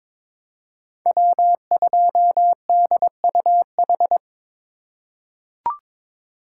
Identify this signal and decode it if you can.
Morse code, heard as a keyed tone, about 23 wpm, 700 Hz. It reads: W2DUH
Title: USA Call Signs - 2x Character Spacing - 22wpm